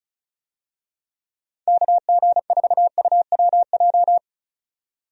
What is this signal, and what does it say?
Morse code, heard as a keyed tone, about 34 wpm, 700 Hz. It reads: KG4UWJ
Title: USA Call Signs - 6 Characters - 35wpm